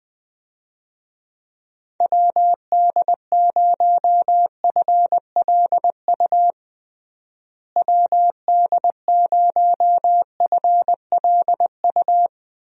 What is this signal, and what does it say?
Morse code, heard as a keyed tone, about 20 wpm, 700 Hz. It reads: WD0FLU WD0FLU
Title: USA Call Signs - 6 Characters - Repeated 3x - 20wpm